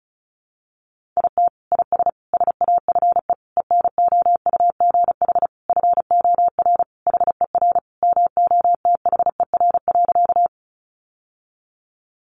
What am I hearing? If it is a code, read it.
Morse code, heard as a keyed tone, about 34 wpm, 700 Hz. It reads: IT IS SAFE ENOUGH FOR HER MOTHER.
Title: Sentences from Top 900 Words 35wpm